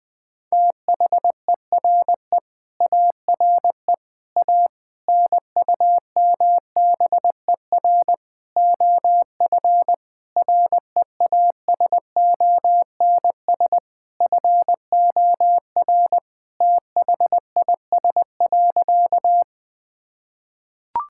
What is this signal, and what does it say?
Morse code, heard as a keyed tone, about 21 wpm, 700 Hz. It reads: THERE ARE A NUMBER OF REASONS FOR THIS.
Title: Sentences with 5-Word Phrases - 20wpm